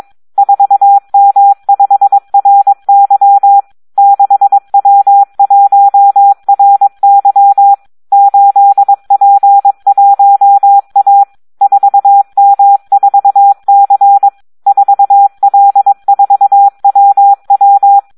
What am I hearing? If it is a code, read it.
Morse code, heard as a keyed tone, about 22 wpm, 800 Hz. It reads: 4M5RY 6W1RY 8P1A 4M4C 4L4WW
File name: Audio practica CW / SD 1 22